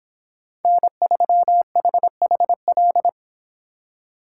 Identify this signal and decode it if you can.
Morse code, heard as a keyed tone, about 26 wpm, 700 Hz. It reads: N3HHL